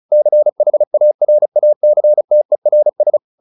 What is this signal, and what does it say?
Morse code, heard as a keyed tone, about 36 wpm, 600 Hz. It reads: CHARACTERS